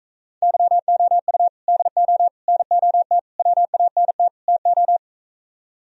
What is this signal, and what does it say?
Morse code, heard as a keyed tone, about 45 wpm, 700 Hz. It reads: YOU DO NOT WANT TO